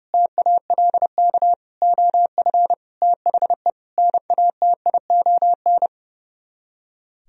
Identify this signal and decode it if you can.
Morse code, heard as a keyed tone, about 40 wpm, 700 Hz. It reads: TALK OF THE NATION